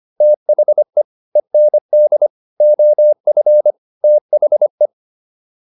Morse code, heard as a keyed tone, about 25 wpm, 600 Hz. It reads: THE END OF THE